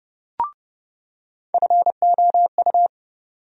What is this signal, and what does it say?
Morse code, heard as a keyed tone, about 30 wpm, 700 Hz. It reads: FOU
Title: USA Call Sign Postfixes - 3 Letters 30wpm